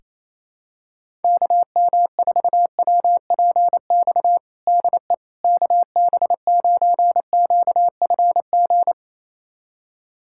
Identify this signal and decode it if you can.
Morse code, heard as a keyed tone, about 28 wpm, 700 Hz. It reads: KM4WPX DE KB9QFG
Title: USA Callsign DE Callsign - 6 Characters - 28wpm